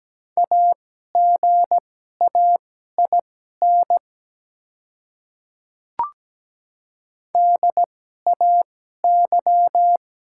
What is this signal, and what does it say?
Morse code, heard as a keyed tone, about 17 wpm, spaced as for 12 wpm, 700 Hz. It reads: AGAIN DAY
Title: 2x Character Spacing - Top 200 Words - 17wpm